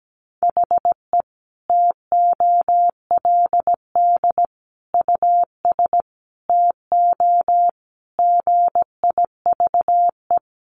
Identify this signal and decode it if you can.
Morse code, heard as a keyed tone, about 17 wpm, 700 Hz. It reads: HE TOLD US TO GIVE
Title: Sentences from Top 1000 Words 17wpm